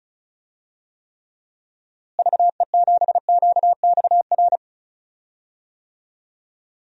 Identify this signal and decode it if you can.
Morse code, heard as a keyed tone, about 36 wpm, 700 Hz. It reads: VE7QXR
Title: DXCC-100 Call Signs - 6 Characters - Repeated 2x - 35wpm